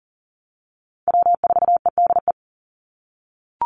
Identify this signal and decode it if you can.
Morse code, heard as a keyed tone, about 40 wpm, 700 Hz. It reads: W4EDE
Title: Call Signs - 40wpm